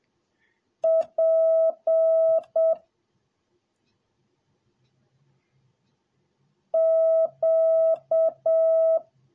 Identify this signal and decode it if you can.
Morse code, heard as a keyed tone, about 7 wpm, 650 Hz. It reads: P Q